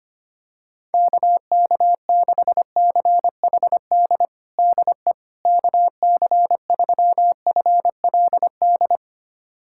Morse code, heard as a keyed tone, about 25 wpm, 700 Hz. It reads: KK6CHD DE KC3FLD
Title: USA Callsign DE Callsign - 6 Characters - 25wpm